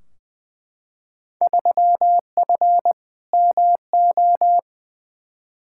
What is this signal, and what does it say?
Morse code, heard as a keyed tone, about 20 wpm, 700 Hz. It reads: 3F MO
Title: Field Day Exchanges Encore - 20wpm